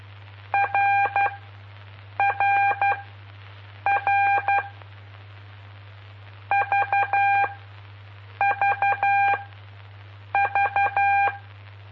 Morse code, heard as a keyed tone, about 12 wpm, spaced as for 7 wpm, 800 Hz. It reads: RRR VVV
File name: pp5bar / aulas de telegrafia / Aula-05